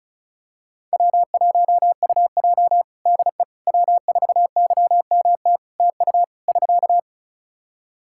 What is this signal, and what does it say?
Morse code, heard as a keyed tone, about 36 wpm, 700 Hz. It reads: W1UJ DE W4YMT TU <SK>
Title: Practice QSO - Exchange 4 - Ending - 35wpm